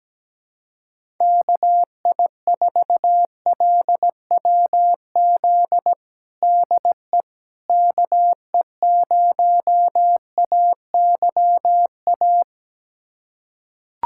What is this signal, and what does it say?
Morse code, heard as a keyed tone, about 17 wpm, 700 Hz. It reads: KI4LWZ DE KE0AYA
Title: USA Callsign DE Callsign - 6 Characters - 17wpm